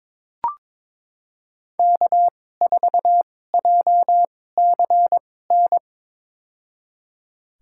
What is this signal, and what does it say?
Morse code, heard as a keyed tone, about 22 wpm, spaced as for 16 wpm, 700 Hz. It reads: K4JCN
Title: USA Call Signs - 5 Characters - 2x Character Spacing - 22wpm